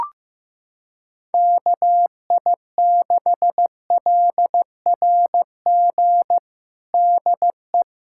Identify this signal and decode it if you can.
Morse code, heard as a keyed tone, about 15 wpm, 700 Hz. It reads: KI6LRG DE
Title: USA Callsign DE Callsign - 6 Characters - 15wpm